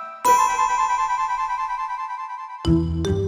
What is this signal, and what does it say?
There is no code.